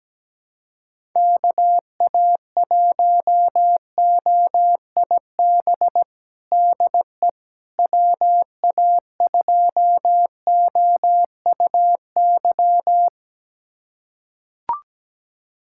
Morse code, heard as a keyed tone, about 17 wpm, 700 Hz. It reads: KA1OIB DE WA2OUY
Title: USA Callsign DE Callsign 17wpm